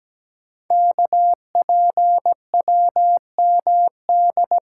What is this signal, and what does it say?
Morse code, heard as a keyed tone, about 17 wpm, 700 Hz. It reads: KPWMD
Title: Code Groups - 5 Letters - 17wpm